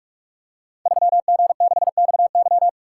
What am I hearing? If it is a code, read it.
Morse code, heard as a keyed tone, about 44 wpm, 700 Hz. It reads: 3G6XY
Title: Code Groups - 5 Letter:Numbers - 45wpm